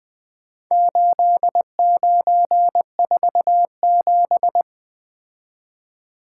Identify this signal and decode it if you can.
Morse code, heard as a keyed tone, about 20 wpm, 700 Hz. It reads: 8947
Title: Four Letter-Number Combinations - 20wpm